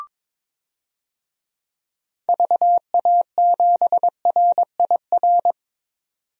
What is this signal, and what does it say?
Morse code, heard as a keyed tone, about 22 wpm, 700 Hz. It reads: VA7RIR
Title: DXCC-100 Call Signs - 6 Characters - 22wpm